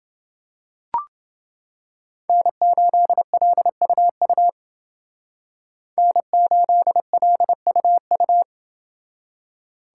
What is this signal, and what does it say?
Morse code, heard as a keyed tone, about 28 wpm, 700 Hz. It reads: N8LUU N8LUU